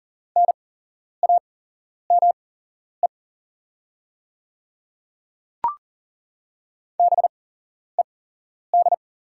Morse code, heard as a keyed tone, about 40 wpm, spaced as for 11 wpm, 690 Hz. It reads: NAME BED
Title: 8x Character Spacing - Top 500 Words - 40wpm